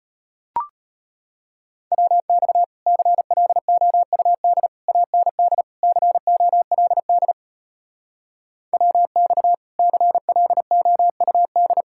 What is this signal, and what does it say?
Morse code, heard as a keyed tone, about 33 wpm, 700 Hz. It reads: WX CLOUD AND COLD WX CLOUD